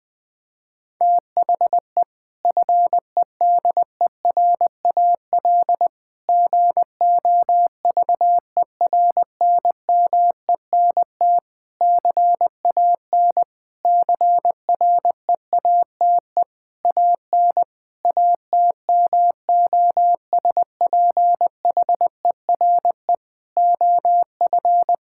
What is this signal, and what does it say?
Morse code, heard as a keyed tone, about 20 wpm, 700 Hz. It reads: THE FEDERAL GOVERNMENT CAN CREATE AN ATMOSPHERE OF
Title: Sentences from Top 1600 Words 20wpm